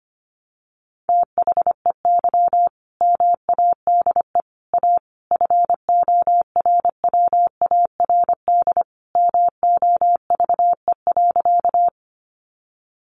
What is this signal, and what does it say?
Morse code, heard as a keyed tone, about 25 wpm, 700 Hz. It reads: THEY MADE A FORWARD MOVE.
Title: Sentences from Top 1300 Words 25wpm